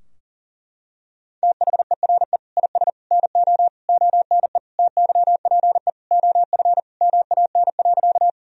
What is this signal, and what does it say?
Morse code, heard as a keyed tone, about 40 wpm, 700 Hz. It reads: THERE IS NO ONE TYPE OF MAN.
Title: Sentences from Top 1000 Words - Review Part 3 - 40wpm